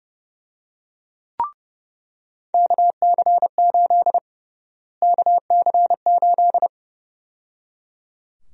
Morse code, heard as a keyed tone, about 30 wpm, 700 Hz. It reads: KC8 KC8
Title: USA Call Sign Prefixes - Repeated 2x - 30wpm